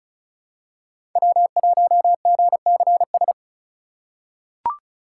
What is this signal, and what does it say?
Morse code, heard as a keyed tone, about 36 wpm, 700 Hz. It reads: W1GCS
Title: K1USN SST QSO Practice - 35wpm